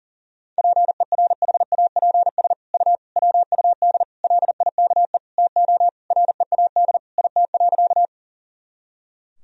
Morse code, heard as a keyed tone, about 40 wpm, 700 Hz. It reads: PERHAPS U WUD LIKE TO READ IT.
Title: Sentences from Top 500 Words and Abbreviations 40wpm